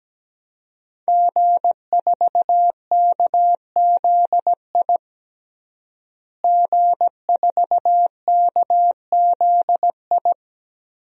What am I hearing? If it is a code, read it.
Morse code, heard as a keyed tone, about 17 wpm, 700 Hz. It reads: G4KZI G4KZI
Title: DXCC-100 Call Signs - 5 Characters - Repeated 2x - 17wpm